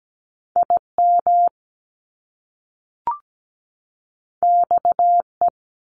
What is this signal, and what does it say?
Morse code, heard as a keyed tone, about 17 wpm, 700 Hz. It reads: IM XE
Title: Top Two Letter Combinations in Words 17wpm